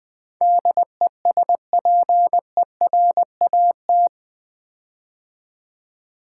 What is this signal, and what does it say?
Morse code, heard as a keyed tone, about 20 wpm, 700 Hz. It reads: DESPERAT
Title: Build A Word - 4 Letters At A Time - 11-Letter Words - 20wpm